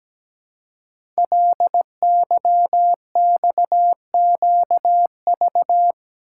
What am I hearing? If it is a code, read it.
Morse code, heard as a keyed tone, about 17 wpm, 700 Hz. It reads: LYXQV
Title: Code Groups - 5 Letters - 17wpm